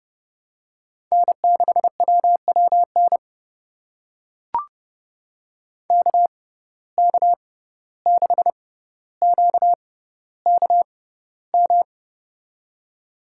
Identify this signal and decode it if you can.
Morse code, heard as a keyed tone, about 30 wpm, spaced as for 15 wpm, 700 Hz. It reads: N6WWN KK6QKM
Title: USA Call Signs - 6x Character Spacing - 30wpm